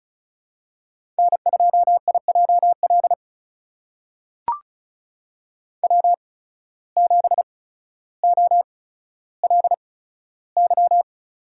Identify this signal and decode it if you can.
Morse code, heard as a keyed tone, about 36 wpm, spaced as for 15 wpm, 700 Hz. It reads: N2IJL W7OLY
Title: USA Call Signs - 5 Characters - 8x Character Spacing - 35wpm